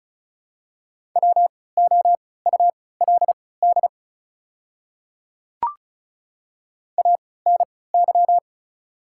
Morse code, heard as a keyed tone, about 36 wpm, spaced as for 20 wpm, 690 Hz. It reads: WOULD ANY
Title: 3x Character Spacing - Top 200 Words - 35wpm